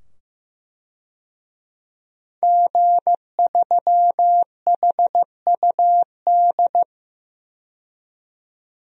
Morse code, heard as a keyed tone, about 15 wpm, 700 Hz. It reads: G3HUD
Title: DXCC-100 Call Signs - 5 Characters - 15wpm